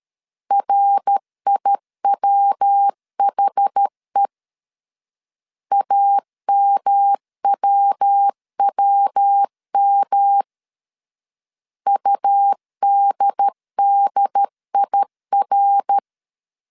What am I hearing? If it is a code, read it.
Morse code, heard as a keyed tone, about 13 wpm, 800 Hz. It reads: RIWHE AMWWM UDDIR
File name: CW / DET / 1 3 1500C S B 12wpm 2828